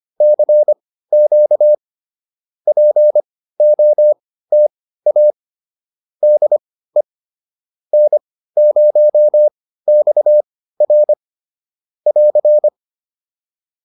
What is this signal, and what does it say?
Morse code, heard as a keyed tone, about 25 wpm, spaced as for 15 wpm, 600 Hz. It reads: CQ POTA DE N0XR <AR>
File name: POTA203 15